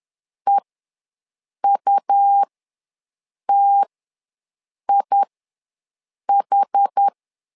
Morse code, heard as a keyed tone, about 10 wpm, spaced as for 6 wpm, 800 Hz. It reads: EUTIH
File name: LEZIONI / 2 UHDN / 2 UH ws 10wpm 2029